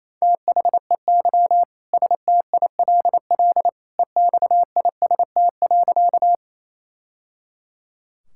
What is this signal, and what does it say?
Morse code, heard as a keyed tone, about 28 wpm, 700 Hz. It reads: THEY STILL EXIST.
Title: Sentences from Top 1000 Words - Review Part 5 - 28wpm